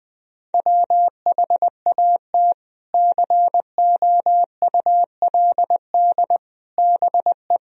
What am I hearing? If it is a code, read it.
Morse code, heard as a keyed tone, about 20 wpm, 700 Hz. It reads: WHAT COULD BE
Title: Sentences from Top 400 Words and Abbreviations 20wpm